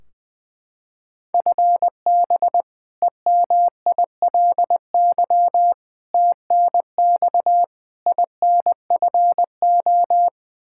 Morse code, heard as a keyed tone, about 20 wpm, 700 Hz. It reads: FB EMILY TNX INFO
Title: Practice QSO - Exchange 3 - More Info - 20wpm